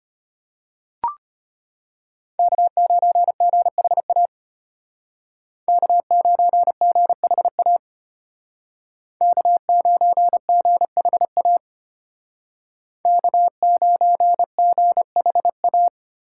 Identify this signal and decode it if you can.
Morse code, heard as a keyed tone, about 31 wpm, 700 Hz. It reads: K9GHA K9GHA K9GHA K9GHA
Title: Speed Racing - USA Call Signs 25wpm